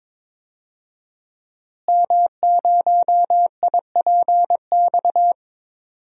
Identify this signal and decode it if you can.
Morse code, heard as a keyed tone, about 22 wpm, 700 Hz. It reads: M0IPX
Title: DXCC-100 Call Signs - 5 Characters - Repeated 2x - 22wpm